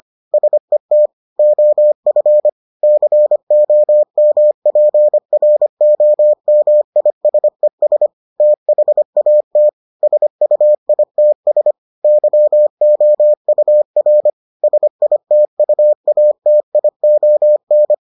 Morse code, heard as a keyed tone, about 25 wpm, 600 Hz. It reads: SET OF COMPROMISES THAT SUITS YOUR SITUATION